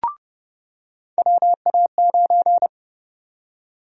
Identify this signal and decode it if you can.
Morse code, heard as a keyed tone, about 30 wpm, 700 Hz. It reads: WA9
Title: USA Call Sign Prefixes - 30wpm